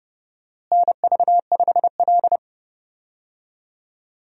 Morse code, heard as a keyed tone, about 29 wpm, 700 Hz. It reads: NV5L
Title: USA Call Signs - 4 Characters - Repeated 3x - 30wpm